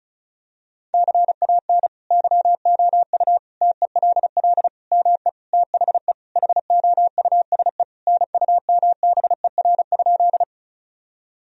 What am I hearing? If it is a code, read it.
Morse code, heard as a keyed tone, about 34 wpm, 700 Hz. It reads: CAN YOU TELL ME THE HOUSE NUMBER?
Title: Sentences from Top 1000 Words - Review Part 1 - 35wpm